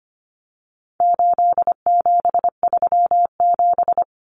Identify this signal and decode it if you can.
Morse code, heard as a keyed tone, about 25 wpm, 700 Hz. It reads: 8737
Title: Common Cut Numbers - 4-Digit 25wpm